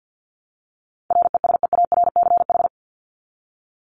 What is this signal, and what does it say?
Morse code, heard as a keyed tone, about 50 wpm, 700 Hz. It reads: RESEARCH
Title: Top 501-750 Words 50wpm